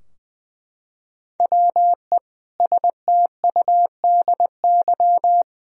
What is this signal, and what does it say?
Morse code, heard as a keyed tone, about 20 wpm, 700 Hz. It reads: WE STUDY
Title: Sentences from Top 1000 Words - Review Part 3 - 20wpm